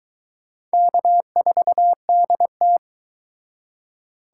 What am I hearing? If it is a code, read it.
Morse code, heard as a keyed tone, about 22 wpm, 700 Hz. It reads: K4DT